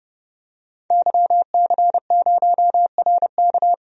Morse code, heard as a keyed tone, about 31 wpm, 700 Hz. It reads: YC0RK
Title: DXCC-100 Call Signs - 5 Characters - 30wpm